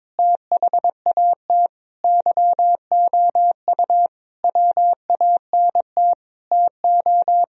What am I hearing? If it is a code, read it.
Morse code, heard as a keyed tone, about 22 wpm, 700 Hz. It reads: THAT YOU WANT TO
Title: Sentences from Top 1000 Words - Review Part 2 - 22wpm